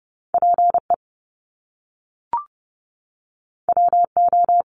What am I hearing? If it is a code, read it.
Morse code, heard as a keyed tone, about 30 wpm, 700 Hz. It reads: PE WO
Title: Top Two Letter Combinations in Words 30wpm